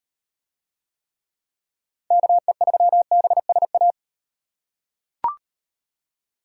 Morse code, heard as a keyed tone, about 35 wpm, 700 Hz. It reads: KE3BSA